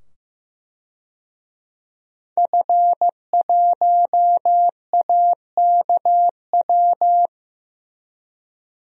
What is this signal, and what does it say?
Morse code, heard as a keyed tone, about 15 wpm, 700 Hz. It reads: F1AKW
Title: DXCC-100 Call Signs - 5 Characters - 15wpm